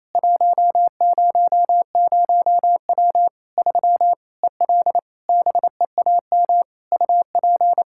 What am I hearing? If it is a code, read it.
Morse code, heard as a keyed tone, about 28 wpm, 700 Hz. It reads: 100W 3 EL BEAM UP
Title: Practice QSO - Exchange 2 - Info - 28wpm